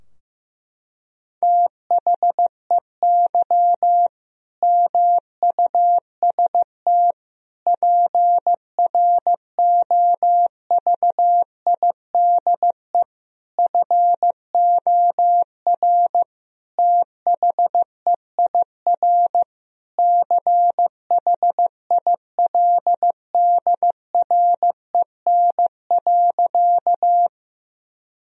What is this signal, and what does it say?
Morse code, heard as a keyed tone, about 16 wpm, 700 Hz. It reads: THEY MUST PROVIDE FOR THEIR CHILDREN.
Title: Sentences from Top 1000 Words - Review Part 3 - 15wpm